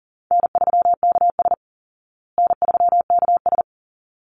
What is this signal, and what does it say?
Morse code, heard as a keyed tone, about 40 wpm, 700 Hz. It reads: N3KS N3KS
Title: Easy Call Signs Repeated 2x - 40wpm